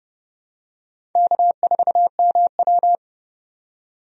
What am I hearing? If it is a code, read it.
Morse code, heard as a keyed tone, about 29 wpm, 700 Hz. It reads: K4MW
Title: USA Call Signs - 4 Characters - 30wpm